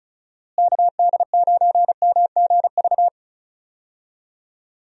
Morse code, heard as a keyed tone, about 36 wpm, 700 Hz. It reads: KD9MGV
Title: USA Call Signs - 6 Characters - Repeated 2x - 35wpm